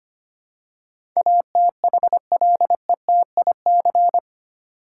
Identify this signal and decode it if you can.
Morse code, heard as a keyed tone, about 25 wpm, 700 Hz. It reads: ATHLETIC
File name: Eight Letter Words - 25wpm